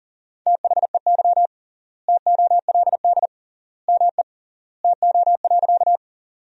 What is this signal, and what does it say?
Morse code, heard as a keyed tone, about 40 wpm, 700 Hz. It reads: THEY TOLD ME TO.
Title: 3x Word-Spacing - Sentences from Top 400 Words - 40wpm